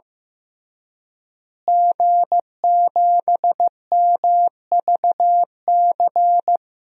Morse code, heard as a keyed tone, about 15 wpm, 700 Hz. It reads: G7MVC
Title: DXCC-100 Call Signs - 5 Characters - Repeated 3x - 15wpm